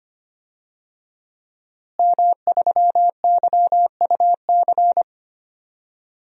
Morse code, heard as a keyed tone, about 25 wpm, 700 Hz. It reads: M3YUC
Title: DXCC-100 Call Signs - 5 Characters - Repeated 3x - 25wpm